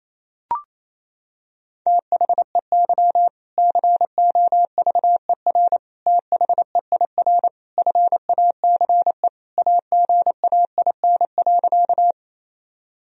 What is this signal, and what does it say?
Morse code, heard as a keyed tone, about 28 wpm, 700 Hz. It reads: THEY COVER THEIR FACE AGAIN.
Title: Sentences from Top 1000 Words - Review Part 1 - 28wpm